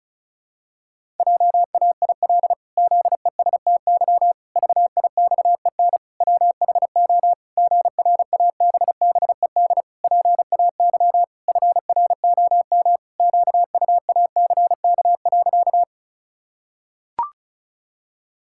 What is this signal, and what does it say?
Morse code, heard as a keyed tone, about 36 wpm, 700 Hz. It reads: JAIL ZESTY VIXEN WHO GRABBED PAY FROM QUACK.
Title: Pangram Challenge v2 - 35wpm